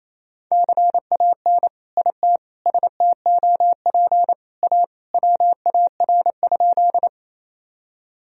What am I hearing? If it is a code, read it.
Morse code, heard as a keyed tone, about 28 wpm, 700 Hz. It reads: CAN IT STOP A WAR?
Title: Sentences from Top 1000 Words - Review Part 2 - 28wpm